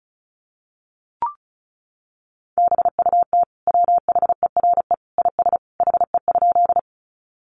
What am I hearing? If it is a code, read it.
Morse code, heard as a keyed tone, about 34 wpm, 700 Hz. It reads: BUT WHERE IS HE?
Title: Sentences from Top 200 Words 35wpm